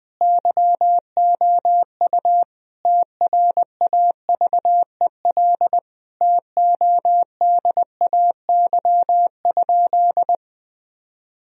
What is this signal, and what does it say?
Morse code, heard as a keyed tone, about 21 wpm, 700 Hz. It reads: YOU TRAVEL TODAY?
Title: Sentences from Top 1000 Words - Review Part 2 - 20wpm